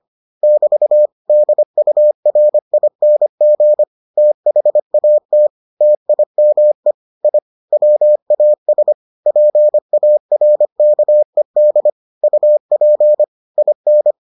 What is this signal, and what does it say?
Morse code, heard as a keyed tone, about 25 wpm, 600 Hz. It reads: <BT> DURING THAT TIME I WAS PARKED UP IN